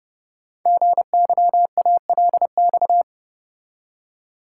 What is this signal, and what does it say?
Morse code, heard as a keyed tone, about 30 wpm, 700 Hz. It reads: GYALX
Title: Code Groups - 5 Letters - 30wpm